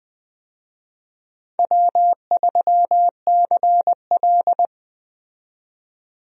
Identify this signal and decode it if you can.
Morse code, heard as a keyed tone, about 20 wpm, 700 Hz. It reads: W3CL
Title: USA Call Signs - 4 Characters - 20wpm